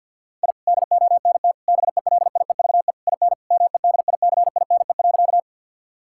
Morse code, heard as a keyed tone, about 46 wpm, 700 Hz. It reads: I DONT BELIEVE IN MEDICINE.
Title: Sentences from Top 1000 Words - Review Part 6 - 50wpm